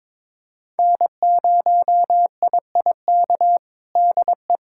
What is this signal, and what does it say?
Morse code, heard as a keyed tone, about 22 wpm, 700 Hz. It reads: N0IIK DE
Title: USA Callsign DE Callsign - 5 Characters - 22wpm